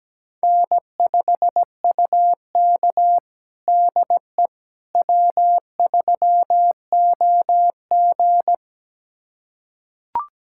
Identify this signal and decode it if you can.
Morse code, heard as a keyed tone, about 17 wpm, 700 Hz. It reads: N5UK DE W3OG
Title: USA Callsign DE Callsign - 4 Characters - 17wpm